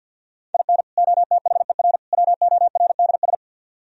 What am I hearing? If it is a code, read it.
Morse code, heard as a keyed tone, about 48 wpm, 700 Hz. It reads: IN OTHER WORDS
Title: Speed-Racing - Sets of 3 Words 50wpm